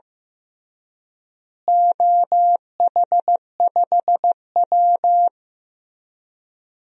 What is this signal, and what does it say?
Morse code, heard as a keyed tone, about 15 wpm, 700 Hz. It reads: OH5W
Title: DXCC-100 Call Signs - 4 Characters - Repeated 3x - 15wpm